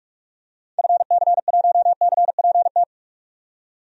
Morse code, heard as a keyed tone, about 44 wpm, 700 Hz. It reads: FC1CPT
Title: DXCC-100 Call Signs - 6 Characters - 45wpm